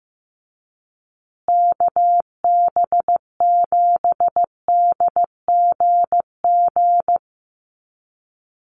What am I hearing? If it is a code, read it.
Morse code, heard as a keyed tone, about 15 wpm, 700 Hz. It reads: KB7DGG
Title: USA Call Signs Repeated 2x 15wpm